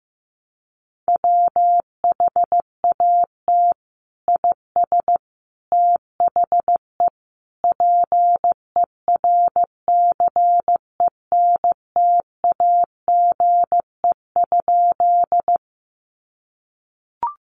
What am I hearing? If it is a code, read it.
Morse code, heard as a keyed tone, about 15 wpm, 700 Hz. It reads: WHAT IS THE PERCENTAGE?
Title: Sentences from Top 1300 Words 15wpm